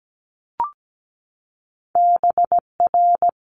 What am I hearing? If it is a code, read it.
Morse code, heard as a keyed tone, about 17 wpm, 700 Hz. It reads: BR
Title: Multi-letter Phonemes 17wpm